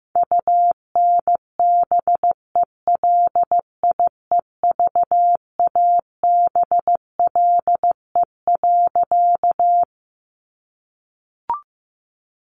Morse code, heard as a keyed tone, about 16 wpm, 700 Hz. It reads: UNBELIEVABLE.
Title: A Sound Of Thunder 15wpm